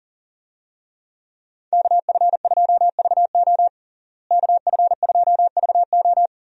Morse code, heard as a keyed tone, about 40 wpm, 700 Hz. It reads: KF2VO KF2VO
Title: USA Call Signs - 5 Characters - Repeated 3x - 40wpm